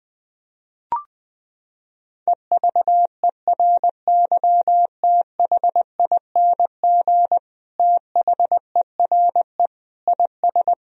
Morse code, heard as a keyed tone, about 21 wpm, 700 Hz. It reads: EVERYTHING THERE IS
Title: Sentences from Top 1000 Words - Review Part 1 - 20wpm